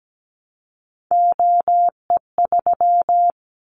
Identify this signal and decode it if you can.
Morse code, heard as a keyed tone, about 17 wpm, 700 Hz. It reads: OE3
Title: Callsign Prefixes for Top 100 DXCC - 17wpm